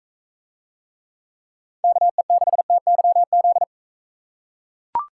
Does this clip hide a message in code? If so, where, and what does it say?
Yes: Morse code from 1.8 to 3.6 s, heard as a keyed tone, about 43 wpm, 700 Hz. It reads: KE6TYZ